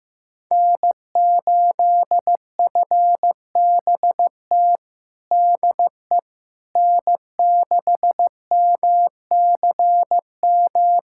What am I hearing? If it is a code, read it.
Morse code, heard as a keyed tone, about 15 wpm, 700 Hz. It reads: N8FBT DE N6MCM
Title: USA Callsign DE Callsign - 5 Characters - 15wpm